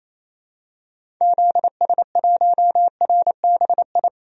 Morse code, heard as a keyed tone, about 28 wpm, 700 Hz. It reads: ZS1RBI
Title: DXCC-100 Call Signs - 6 Characters - 28wpm